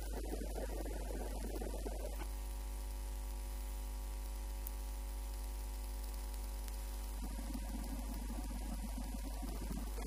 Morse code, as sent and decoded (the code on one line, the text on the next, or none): none